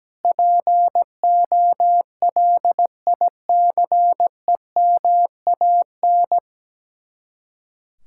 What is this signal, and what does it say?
Morse code, heard as a keyed tone, about 17 wpm, 700 Hz. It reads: POLICEMAN
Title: Speed-Racing Common QSO Phrases - 17wpm